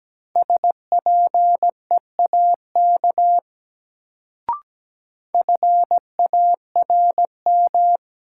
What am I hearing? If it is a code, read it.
Morse code, heard as a keyed tone, about 17 wpm, 700 Hz. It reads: SPEAK FARM
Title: Sentences from Top 1000 Words - Review Part 2 - New Word Drill - 17wpm